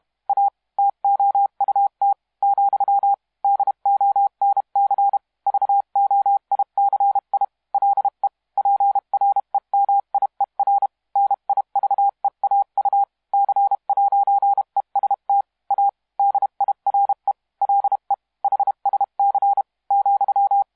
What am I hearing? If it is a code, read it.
Morse code, heard as a keyed tone, about 31 wpm, 800 Hz. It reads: A TOUT , DONC VOICI LE PREMIER NIVEAU C'EST A DIRE LE HSC ,
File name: musiqueradio / cw hsc demo f6ddr